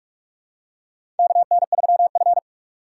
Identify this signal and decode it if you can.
Morse code, heard as a keyed tone, about 46 wpm, 700 Hz. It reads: KN3F
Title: USA Call Signs - 4 Characters - 4x Character Spacing - 45wpm